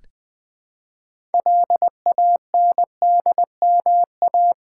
Morse code, heard as a keyed tone, about 21 wpm, 700 Hz. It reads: LANDMA
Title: Build A Word - 3 Letters At A Time - 8-Letter Words - 20wpm